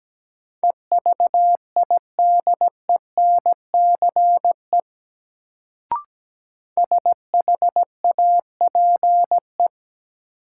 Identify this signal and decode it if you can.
Morse code, heard as a keyed tone, about 17 wpm, 700 Hz. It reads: EVIDENCE SHAPE
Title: Sets of 5 Words Encore - New Word Drill - 17wpm